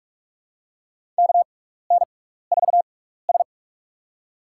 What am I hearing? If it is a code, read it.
Morse code, heard as a keyed tone, about 41 wpm, spaced as for 15 wpm, 700 Hz. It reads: KN4S
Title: USA Call Signs - 6x Character Spacing - 45wpm